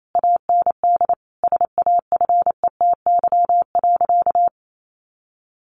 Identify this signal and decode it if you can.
Morse code, heard as a keyed tone, about 28 wpm, 700 Hz. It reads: AND SAFETY.
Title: A Sound Of Thunder 28wpm